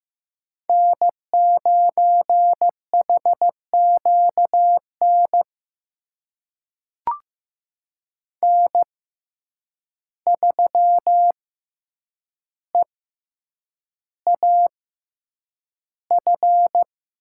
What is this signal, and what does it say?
Morse code, heard as a keyed tone, about 15 wpm, spaced as for 8 wpm, 700 Hz. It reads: N9HQN N3EAF
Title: USA Call Signs - 6x Character Spacing - 15wpm